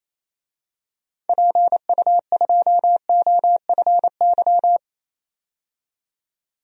Morse code, heard as a keyed tone, about 28 wpm, 700 Hz. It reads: PU2OFY
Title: DXCC-100 Call Signs - 6 Characters - 28wpm